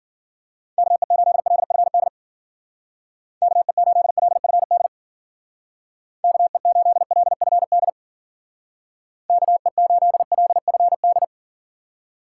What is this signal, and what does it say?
Morse code, heard as a keyed tone, about 54 wpm, 700 Hz. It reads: KE8LFD KE8LFD KE8LFD KE8LFD